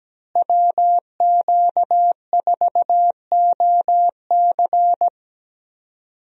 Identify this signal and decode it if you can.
Morse code, heard as a keyed tone, about 17 wpm, 700 Hz. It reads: WQ4OC
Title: USA Call Signs - 5 Characters - Repeated 3x - 17wpm